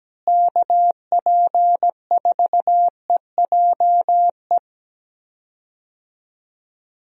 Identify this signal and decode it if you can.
Morse code, heard as a keyed tone, about 17 wpm, 700 Hz. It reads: KP4EJE
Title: DXCC-100 Call Signs - 6 Characters - Repeated 3x - 17wpm